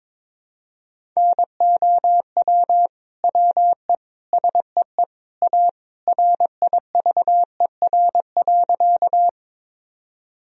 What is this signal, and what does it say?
Morse code, heard as a keyed tone, about 22 wpm, 700 Hz. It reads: NOW WE SEE A RIVER.
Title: Sentences from Top 1000 Words - Review Part 1 - 22wpm